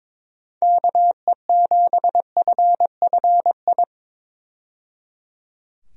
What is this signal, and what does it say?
Morse code, heard as a keyed tone, about 22 wpm, 700 Hz. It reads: KE7FFI
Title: USA Call Signs - 6 Characters - 22wpm